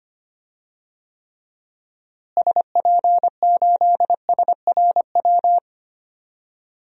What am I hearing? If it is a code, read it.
Morse code, heard as a keyed tone, about 25 wpm, 700 Hz. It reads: SP8SRW
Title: DXCC-100 Call Signs - 6 Characters - Repeated 3x - 25wpm